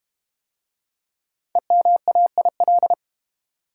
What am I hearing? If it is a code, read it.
Morse code, heard as a keyed tone, about 30 wpm, 700 Hz. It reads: EMAIL
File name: Easy Words - Speed-Racing with Hints - 25wpm